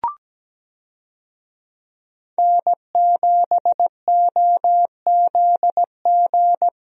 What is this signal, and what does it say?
Morse code, heard as a keyed tone, about 17 wpm, 700 Hz. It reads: N7OZG
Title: USA Call Signs - 5 Characters - Repeated 3x - 17wpm